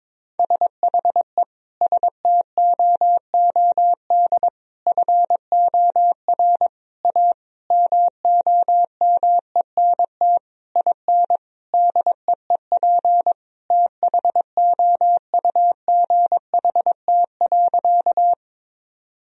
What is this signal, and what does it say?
Morse code, heard as a keyed tone, about 22 wpm, 700 Hz. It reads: SHE STOOD FOR A MOMENT IN DEEP THOUGHT.
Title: Sentences from Top 1000 Words - Review Part 3 - 22wpm